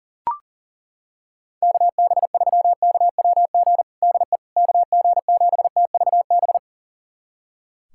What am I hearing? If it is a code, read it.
Morse code, heard as a keyed tone, about 37 wpm, 700 Hz. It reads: KB3KWG DE KG7TVB
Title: USA Callsign DE Callsign 40wpm